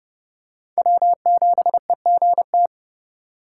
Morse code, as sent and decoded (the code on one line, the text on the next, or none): .-- --... --. -
W7EGT